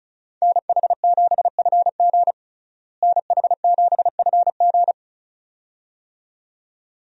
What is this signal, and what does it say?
Morse code, heard as a keyed tone, about 34 wpm, 700 Hz. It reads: NH7FG NH7FG
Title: USA Call Signs - 5 Characters - Repeated 2x - 35wpm